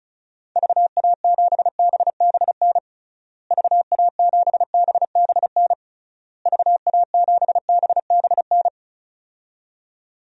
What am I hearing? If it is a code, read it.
Morse code, heard as a keyed tone, about 36 wpm, 700 Hz. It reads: VA7BBN VA7BBN VA7BBN
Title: DXCC-100 Call Signs - 6 Characters - Repeated 3x - 35wpm